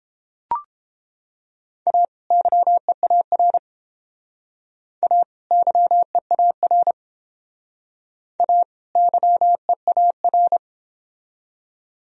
Morse code, heard as a keyed tone, about 30 wpm, 700 Hz. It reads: A YEAR A YEAR A YEAR